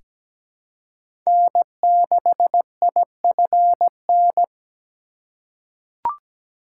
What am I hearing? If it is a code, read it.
Morse code, heard as a keyed tone, about 17 wpm, 700 Hz. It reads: N6IFN
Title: USA Call Signs - 5 Characters - 2x Character Spacing - 17wpm